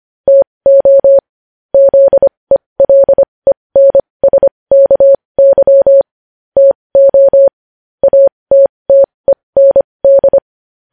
Morse code, heard as a keyed tone, about 25 wpm, spaced as for 20 wpm, 550 Hz. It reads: TO ZELENSKY TO ATTEND
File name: podcast / News headlines-1x-25x20